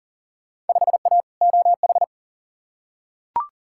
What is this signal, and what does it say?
Morse code, heard as a keyed tone, about 40 wpm, 700 Hz. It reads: 5A OH
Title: Field Day Exchanges Encore - 40wpm